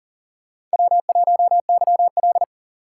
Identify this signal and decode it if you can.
Morse code, heard as a keyed tone, about 37 wpm, 700 Hz. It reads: W1YL
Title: USA Call Signs - 4 Characters - 4x Character Spacing - 40wpm